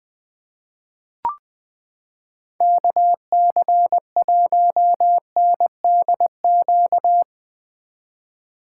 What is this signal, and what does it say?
Morse code, heard as a keyed tone, about 20 wpm, 700 Hz. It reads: KC1NDQ
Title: 2023 POTA Activator Call Signs - 20wpm